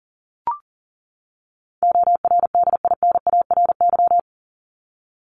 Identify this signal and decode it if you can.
Morse code, heard as a keyed tone, about 40 wpm, 700 Hz. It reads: ORDINARY
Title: Top 1751-2000 Words 40wpm